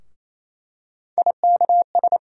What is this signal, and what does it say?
Morse code, heard as a keyed tone, about 28 wpm, 700 Hz. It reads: IKS
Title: USA Call Sign Postfixes - 3 Letters 28wpm